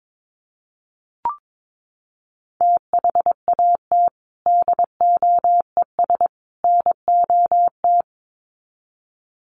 Morse code, heard as a keyed tone, about 22 wpm, 700 Hz. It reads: THAT DOES NOT
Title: Sets of 3 Words 22wpm